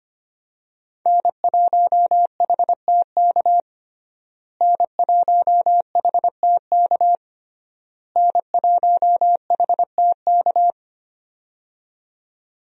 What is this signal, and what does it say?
Morse code, heard as a keyed tone, about 25 wpm, 700 Hz. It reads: N1HTK N1HTK N1HTK
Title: USA Call Signs - 5 Characters - Repeated 3x - 25wpm